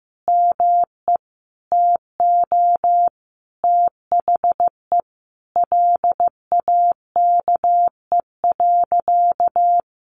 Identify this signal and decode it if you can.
Morse code, heard as a keyed tone, about 15 wpm, 700 Hz. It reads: ME TO THE LAKE.
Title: Sentences from Top 1100 Words 15wpm